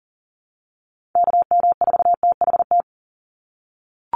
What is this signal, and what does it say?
Morse code, heard as a keyed tone, about 40 wpm, 700 Hz. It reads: KM4THT
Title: USA Call Signs 40wpm